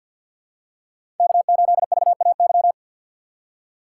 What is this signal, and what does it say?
Morse code, heard as a keyed tone, about 47 wpm, 700 Hz. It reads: K8VAY
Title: USA Call Signs - 5 Characters - 2x Character Spacing - 50wpm